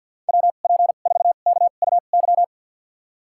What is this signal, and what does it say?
Morse code, heard as a keyed tone, about 49 wpm, spaced as for 35 wpm, 700 Hz. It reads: WP4KUY
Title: USA Call Signs - 2x Character Spacing - 50wpm